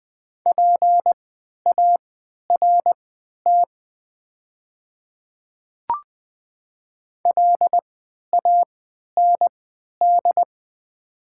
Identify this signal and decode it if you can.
Morse code, heard as a keyed tone, about 20 wpm, spaced as for 11 wpm, 700 Hz. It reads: PART LAND
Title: 3x Character Spacing - Top 200 Words - 20wpm